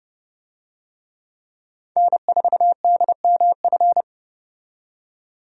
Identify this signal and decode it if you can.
Morse code, heard as a keyed tone, about 30 wpm, 700 Hz. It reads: N4DMF
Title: USA Call Signs - 5 Characters - 30wpm